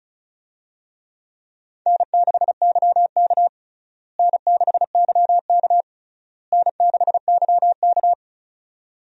Morse code, heard as a keyed tone, about 36 wpm, 700 Hz. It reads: N6YK N6YK N6YK
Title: USA Call Signs - 4 Characters - Repeated 3x - 35wpm